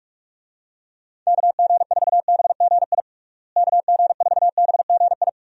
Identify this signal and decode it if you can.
Morse code, heard as a keyed tone, about 45 wpm, 700 Hz. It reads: KG4BGI KG4BGI
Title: USA Call Signs - 6 Characters - Repeated 2x - 45wpm